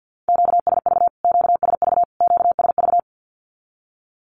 Morse code, heard as a keyed tone, about 48 wpm, 700 Hz. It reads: QSV QSV QSV
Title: Q-Codes Repeated 3x 50wpm